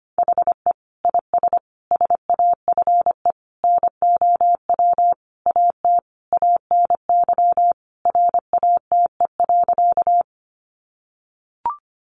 Morse code, heard as a keyed tone, about 25 wpm, 700 Hz. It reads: HE IS SAFE NOW AT ANY RATE.
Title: Sentences from Top 900 Words 25wpm